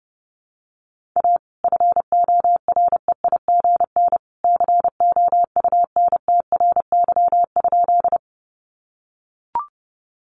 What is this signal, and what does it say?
Morse code, heard as a keyed tone, about 30 wpm, 700 Hz. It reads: A FOREIGN COUNTRY?
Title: Sentences from Top 1400 Words 30wpm